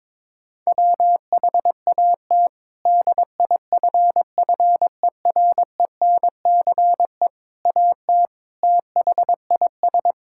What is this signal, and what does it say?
Morse code, heard as a keyed tone, about 22 wpm, 700 Hz. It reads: WHAT DIFFERENCE AT THIS
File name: practice / 22wpm / Sets of 8 Words - 22wpm